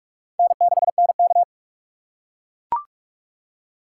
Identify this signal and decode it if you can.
Morse code, heard as a keyed tone, about 44 wpm, 700 Hz. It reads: N6NK
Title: USA Call Signs - 3x Character Spacing - 45wpm